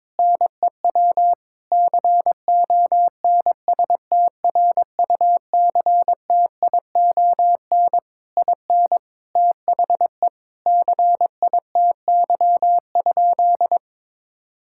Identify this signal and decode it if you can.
Morse code, heard as a keyed tone, about 22 wpm, 700 Hz. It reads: NEW CONSTRUCTION IN THE CITY?
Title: Sentences from Top 1600 Words 22wpm